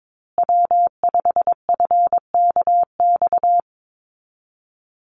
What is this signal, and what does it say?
Morse code, heard as a keyed tone, about 22 wpm, 700 Hz. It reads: W5FKX
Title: Call Signs Repeated 3x - 22wpm